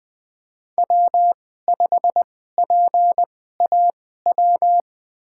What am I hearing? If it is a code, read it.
Morse code, heard as a keyed tone, about 20 wpm, spaced as for 14 wpm, 700 Hz. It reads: W5PAW
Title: USA Call Signs - 2x Character Spacing - 20wpm